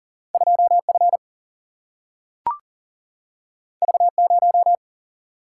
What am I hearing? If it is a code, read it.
Morse code, heard as a keyed tone, about 40 wpm, 700 Hz. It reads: 2F V0
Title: Single-Letter Single-Number 40wpm